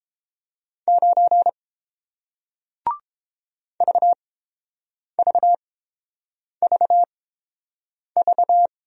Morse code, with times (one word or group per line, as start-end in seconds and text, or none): 0.9-1.5 9
3.8-8.7 VVVV